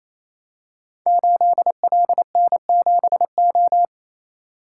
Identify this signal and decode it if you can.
Morse code, heard as a keyed tone, about 29 wpm, 700 Hz. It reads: 8LN7O
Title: Code Groups - 5 Letter:Numbers - 28wpm